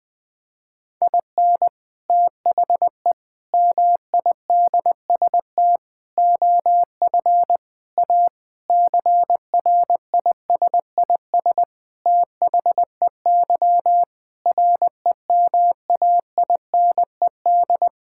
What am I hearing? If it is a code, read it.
Morse code, heard as a keyed tone, about 20 wpm, 700 Hz. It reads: IN THE MIDST OF A CRISIS THEY REMAINED
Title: Sentences with 5-Word Phrases - 20wpm